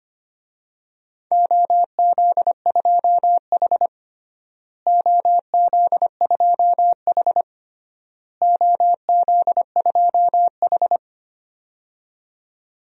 Morse code, heard as a keyed tone, about 25 wpm, 700 Hz. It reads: OZ2H OZ2H OZ2H
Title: DXCC-100 Call Signs - 4 Characters - Repeated 3x - 25wpm